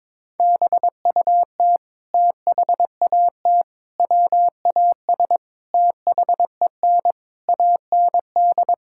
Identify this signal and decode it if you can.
Morse code, heard as a keyed tone, about 22 wpm, 700 Hz. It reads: BUT THAT WAS THEN AND
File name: practice / 22wpm / Sets of 8 Words - 22wpm